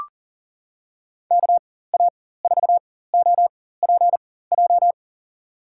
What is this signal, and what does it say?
Morse code, heard as a keyed tone, about 40 wpm, spaced as for 19 wpm, 700 Hz. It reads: KA4OPJ
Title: USA Call Signs - 6 Characters - 4x Character Spacing - 40wpm